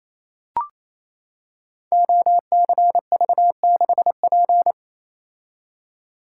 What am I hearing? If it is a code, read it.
Morse code, heard as a keyed tone, about 27 wpm, 690 Hz. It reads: OCV6P
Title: Code Groups - 5 Letter:Numbers - 28wpm